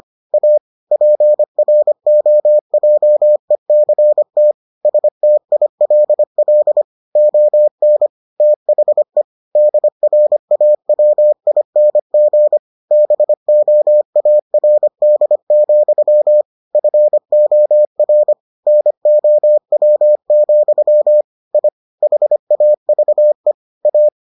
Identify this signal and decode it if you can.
Morse code, heard as a keyed tone, about 25 wpm, 600 Hz. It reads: A PROJECT STILL ON THE DRAWING BOARD, FOR NOW, I HAVE A